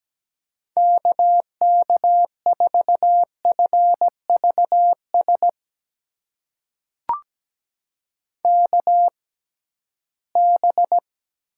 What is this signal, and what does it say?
Morse code, heard as a keyed tone, about 17 wpm, spaced as for 13 wpm, 700 Hz. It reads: KK4FVS KB
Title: USA Call Signs - 6 Characters - 6x Character Spacing - 17wpm